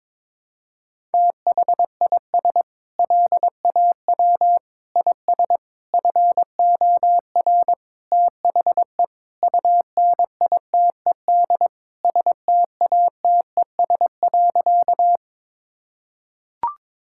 Morse code, heard as a keyed tone, about 22 wpm, 700 Hz. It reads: THIS LAW IS FOR THE UNITED STATES.
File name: practice / 22wpm / Sentences with 4-Word Phrases - 22wpm